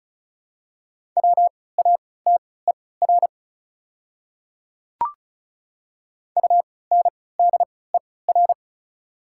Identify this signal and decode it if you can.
Morse code, heard as a keyed tone, about 36 wpm, spaced as for 20 wpm, 690 Hz. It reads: WATER UNDER
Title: 3x Character Spacing - Top 300 Words - 35wpm